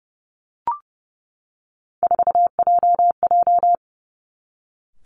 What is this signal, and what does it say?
Morse code, heard as a keyed tone, about 30 wpm, 700 Hz. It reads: 4JJ
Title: Three Challenging Letter-Number Combinations 30wpm